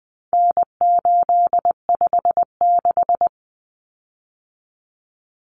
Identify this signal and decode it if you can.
Morse code, heard as a keyed tone, about 20 wpm, 700 Hz. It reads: N856
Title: Common Cut Numbers - 4-Digit 20wpm